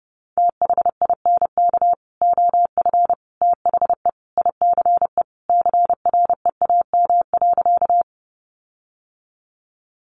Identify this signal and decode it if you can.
Morse code, heard as a keyed tone, about 30 wpm, 700 Hz. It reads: THINK OF THE ICE CREAM.
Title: Sentences from Top 1500 Words 30wpm